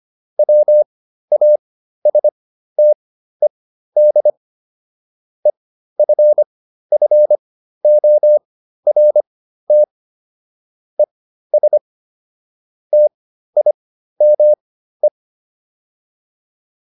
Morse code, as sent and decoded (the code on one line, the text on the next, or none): .-- .- ... - . -.. / . ..-. ..-. --- .-. - / . ... / - .. -- .
WASTED EFFORT ES TIME